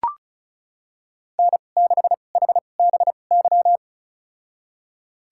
Morse code, heard as a keyed tone, about 36 wpm, spaced as for 25 wpm, 700 Hz. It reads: N6HBY
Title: USA Call Signs - 2x Character Spacing - 35wpm